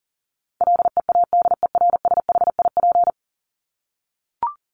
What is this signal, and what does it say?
Morse code, heard as a keyed tone, about 40 wpm, 700 Hz. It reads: LEADERSHIP
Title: Top 1501-1750 Words 40wpm